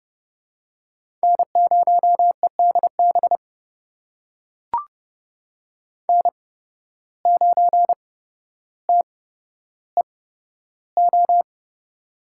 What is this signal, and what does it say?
Morse code, heard as a keyed tone, about 30 wpm, spaced as for 13 wpm, 700 Hz. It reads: N0EDB N9TEO
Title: USA Call Signs - 5 Characters - 8x Character Spacing - 30wpm